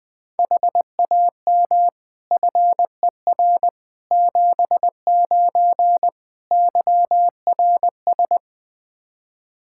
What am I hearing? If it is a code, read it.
Morse code, heard as a keyed tone, about 21 wpm, 700 Hz. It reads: HAM FER 79 YRS
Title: Practice QSO - Exchange 3 - More Info - 20wpm